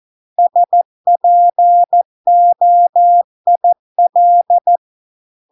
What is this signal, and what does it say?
Morse code, heard as a keyed tone, about 14 wpm, 700 Hz. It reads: SPOIL